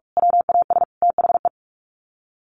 Morse code, heard as a keyed tone, about 44 wpm, 700 Hz. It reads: WAS THE
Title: Sets of 3 Words 45wpm